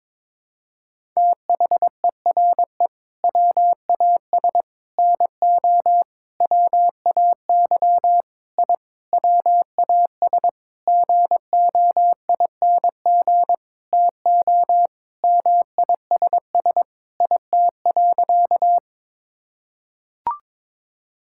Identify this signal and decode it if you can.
Morse code, heard as a keyed tone, about 21 wpm, 700 Hz. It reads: THERE WAS NO WAY I WAS GOING TO MISS IT.
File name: Sentences with 7-Word Phrases - 22wpm